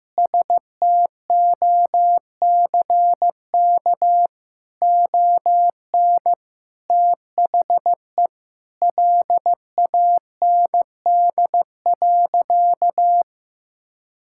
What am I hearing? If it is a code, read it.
Morse code, heard as a keyed tone, about 15 wpm, 700 Hz. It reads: STOCK ON THE LAND.
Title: Sentences from Top 1000 Words - Review Part 5 - 15wpm